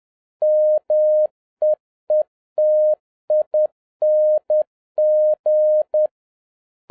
Morse code, heard as a keyed tone, about 10 wpm, 610 Hz. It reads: MEETING